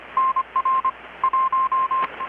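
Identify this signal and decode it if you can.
Morse code, heard as a keyed tone, about 25 wpm, 1050 Hz. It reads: NR 1
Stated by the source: Morse from KPH's Night of Nights XX maritime radiotelegraphy event, received in Chiapas, Mexico about 3,800 km away on a Kenwood R-600 with a grounded 10 m long-wire antenna.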